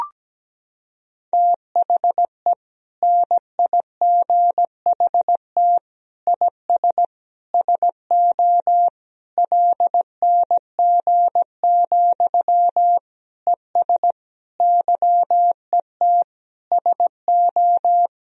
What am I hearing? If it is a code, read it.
Morse code, heard as a keyed tone, about 17 wpm, 700 Hz. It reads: THE NIGHT IS SO LNG, ES YET SO
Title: Sentences from Top 500 Words and Abbreviations 17wpm